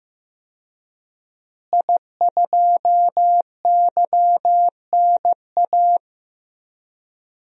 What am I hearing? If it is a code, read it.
Morse code, heard as a keyed tone, about 15 wpm, 700 Hz. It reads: I2YNA
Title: DXCC-100 Call Signs - 5 Characters - Repeated 2x - 15wpm